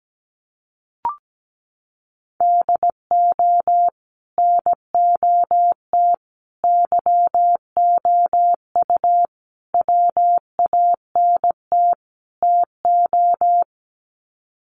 Morse code, heard as a keyed tone, about 17 wpm, 700 Hz. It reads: DO NOT YOU WANT TO
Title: Sets of 5 Words Encore2 17wpm